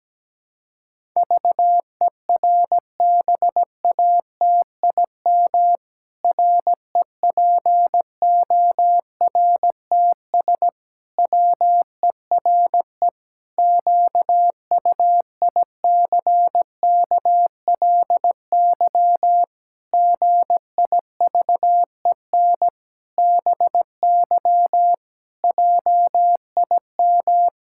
Morse code, heard as a keyed tone, about 17 wpm, 700 Hz. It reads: VERBATIM REPORTS WERE QUICKLY GIVEN BY JIM
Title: Pangram Challenge v2 - 17wpm